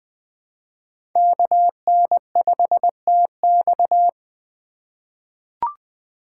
Morse code, heard as a keyed tone, about 21 wpm, 700 Hz. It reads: KN5TX
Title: K1USN SST QSO Practice - 20wpm